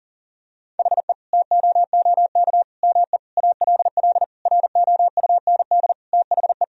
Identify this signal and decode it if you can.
Morse code, heard as a keyed tone, about 40 wpm, 700 Hz. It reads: HE TOOK ME ALL ROUND THE
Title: Sentences from Top 1000 Words - Review Part 1 - 40wpm